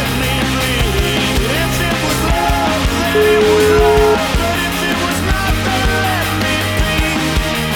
Ringback tone (Europe, 425 Hz), with noise.